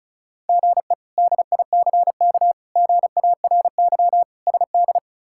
Morse code, heard as a keyed tone, about 36 wpm, 700 Hz. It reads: GE DICK GARY SD
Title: K1USN SST QSO Practice - 35wpm